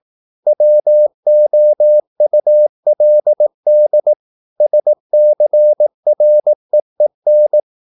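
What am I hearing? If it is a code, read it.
Morse code, heard as a keyed tone, about 18 wpm, 600 Hz. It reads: WOULD SCREEN
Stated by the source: Morse practice at 18 wpm, English text, from CW med Gnister.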